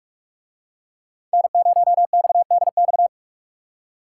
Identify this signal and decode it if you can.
Morse code, heard as a keyed tone, about 44 wpm, 700 Hz. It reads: N0XDX